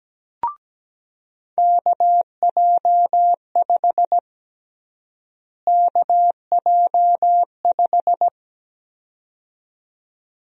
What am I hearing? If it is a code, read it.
Morse code, heard as a keyed tone, about 17 wpm, 700 Hz. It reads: KJ5 KJ5
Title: USA Call Sign Prefixes - Repeated 2x - 17wpm